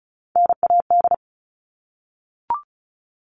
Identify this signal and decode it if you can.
Morse code, heard as a keyed tone, about 34 wpm, 700 Hz. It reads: NAD
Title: All Cut Numbers - 3-Digit 35wpm